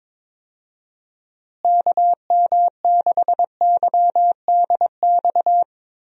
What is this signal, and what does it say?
Morse code, heard as a keyed tone, about 22 wpm, 700 Hz. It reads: KM6YDX
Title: USA Call Signs - 6 Characters - Repeated 3x - 22wpm